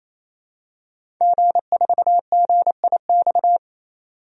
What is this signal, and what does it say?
Morse code, heard as a keyed tone, about 28 wpm, 700 Hz. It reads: G4GIX
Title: DXCC-100 Call Signs - 5 Characters - 28wpm